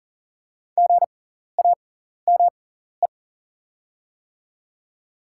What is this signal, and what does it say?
Morse code, heard as a keyed tone, about 40 wpm, spaced as for 14 wpm, 700 Hz. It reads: GAME
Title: 6x Character Spacing - Top 500 Words - 40wpm